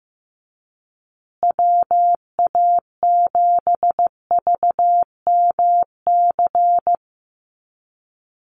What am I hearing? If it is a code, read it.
Morse code, heard as a keyed tone, about 15 wpm, 700 Hz. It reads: WA7VMC
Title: USA Call Signs Repeated 3x 15wpm